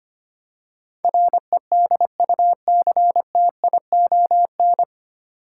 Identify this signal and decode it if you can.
Morse code, heard as a keyed tone, about 25 wpm, 700 Hz. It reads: REDUCTION
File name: practice / 25wpm / 9 Letter Words - First 3 Letters Given - 25wpm